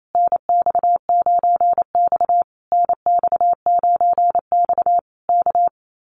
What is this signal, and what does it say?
Morse code, heard as a keyed tone, about 28 wpm, 700 Hz. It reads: NX9X NX9X K
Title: Common QSO Phrases Encore 28wpm